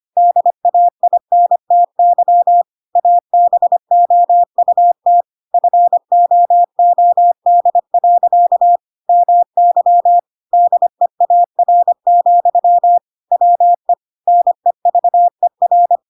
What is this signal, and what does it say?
Morse code, heard as a keyed tone, about 25 wpm, 700 Hz. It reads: DAINTY ABOUT FOOD. MY DEAR, WE NEVER
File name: Pinocchio / Pinocchio ch7 25 25 ik1hge